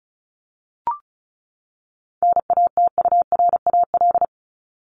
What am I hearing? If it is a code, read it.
Morse code, heard as a keyed tone, about 34 wpm, 700 Hz. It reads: NATURAL
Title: Top 501-750 Words 35wpm